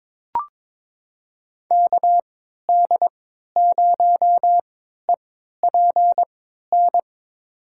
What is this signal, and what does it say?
Morse code, heard as a keyed tone, about 22 wpm, spaced as for 12 wpm, 700 Hz. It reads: KD0EPN
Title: USA Call Signs - 6 Characters - 3x Character Spacing - 22wpm